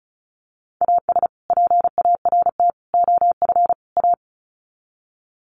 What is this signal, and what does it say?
Morse code, heard as a keyed tone, about 34 wpm, 700 Hz. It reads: AS PART OF A
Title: Sets of 4 Words 35wpm